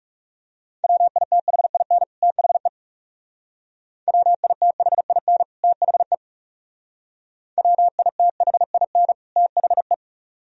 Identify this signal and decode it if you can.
Morse code, heard as a keyed tone, about 37 wpm, 700 Hz. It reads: WITHIN THE WITHIN THE WITHIN THE
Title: Speed-Racing - Sets of 2 Words 30wpm